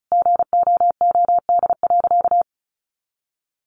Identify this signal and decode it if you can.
Morse code, heard as a keyed tone, about 34 wpm, 700 Hz. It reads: GOOD.
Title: Sentences from Top 1300 Words 35wpm